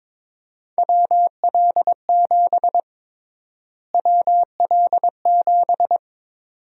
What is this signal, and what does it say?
Morse code, heard as a keyed tone, about 22 wpm, 700 Hz. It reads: WL7 WL7
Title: USA Call Sign Prefixes - Repeated 2x - 22wpm